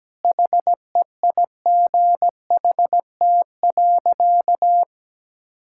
Morse code, heard as a keyed tone, about 17 wpm, 700 Hz. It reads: HEIGHT.
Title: Sentences from Top 1600 Words 17wpm